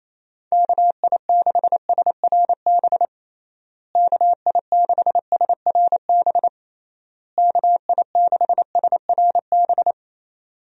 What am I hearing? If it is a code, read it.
Morse code, heard as a keyed tone, about 27 wpm, 700 Hz. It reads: KI6SRB KI6SRB KI6SRB
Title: USA Call Signs - 6 Characters - Repeated 3x - 28wpm